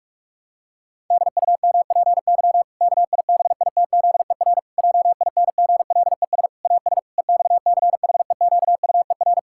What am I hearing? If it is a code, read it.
Morse code, heard as a keyed tone, about 46 wpm, 700 Hz. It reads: DUMPY KIBITZER JINGLES AS EXCHEQUER
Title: Pangram Challenge v2 - 45wpm